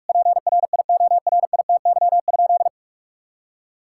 Morse code, heard as a keyed tone, about 43 wpm, 700 Hz. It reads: PRIORITY?
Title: Sentences from Top 1600 Words 45wpm